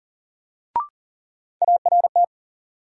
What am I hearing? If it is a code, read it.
Morse code, heard as a keyed tone, about 39 wpm, 700 Hz. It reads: ART
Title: Sentences from Top 600 Words - New Word Drill - 40wpm